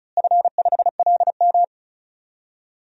Morse code, heard as a keyed tone, about 34 wpm, 700 Hz. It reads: F5LM
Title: DXCC-100 Call Signs - 4 Characters - Repeated 3x - 35wpm